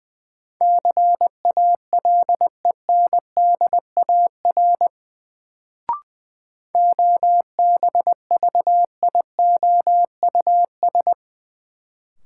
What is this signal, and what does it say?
Morse code, heard as a keyed tone, about 20 wpm, 700 Hz. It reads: CALENDAR OBVIOUS
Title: Sentences from Top 1600 Words - New Word Drill Alt - 20wpm